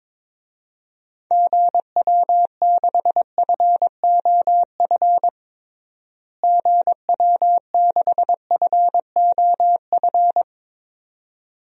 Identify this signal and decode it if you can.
Morse code, heard as a keyed tone, about 21 wpm, 700 Hz. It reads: GW6FOF GW6FOF
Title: DXCC-100 Call Signs - 6 Characters - Repeated 2x - 22wpm